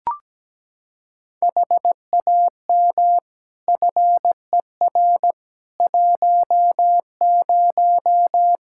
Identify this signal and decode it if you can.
Morse code, heard as a keyed tone, about 17 wpm, 700 Hz. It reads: HAM FER 10
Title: K1USN SST QSO Practice 17wpm